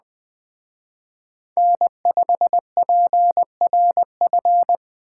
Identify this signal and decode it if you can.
Morse code, heard as a keyed tone, about 20 wpm, 700 Hz. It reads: N5PRF
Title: Speed Racing - USA 1x3 Call Signs 17wpm